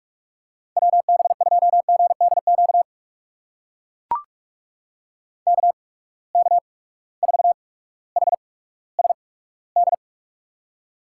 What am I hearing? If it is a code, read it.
Morse code, heard as a keyed tone, about 44 wpm, spaced as for 19 wpm, 700 Hz. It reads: WB2GDQ KK4HSD
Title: USA Call Signs - 8x Character Spacing - 45wpm